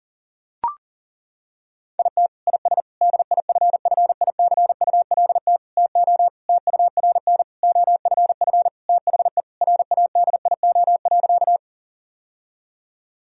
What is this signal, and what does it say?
Morse code, heard as a keyed tone, about 40 wpm, 700 Hz. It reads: IT IS DIFFICULT TO TURN OFF THE RADIO.
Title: Sentences from Top 1000 Words - Review Part 3 - 40wpm